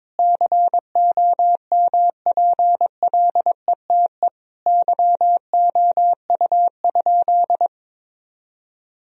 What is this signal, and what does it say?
Morse code, heard as a keyed tone, about 22 wpm, 700 Hz. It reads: COMPLETE YOU?
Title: Sentences from Top 1000 Words - Review Part 2 - 22wpm